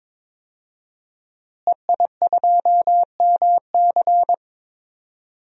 Morse code, heard as a keyed tone, about 22 wpm, 700 Hz. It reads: EI2MC
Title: DXCC-100 Call Signs - 5 Characters - Repeated 2x - 22wpm